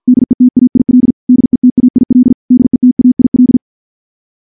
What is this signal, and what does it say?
Morse code, heard as a keyed tone, about 46 wpm, 250 Hz. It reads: DETAIL DETAIL DETAIL
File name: morse / morse-training-50WPM